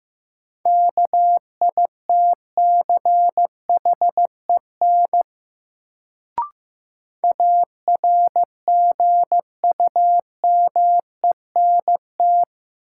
Morse code, heard as a keyed tone, about 15 wpm, 700 Hz. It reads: KITCHEN ARGUMENT
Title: Sentences from Top 1400 Words - New Word Drill - 15wpm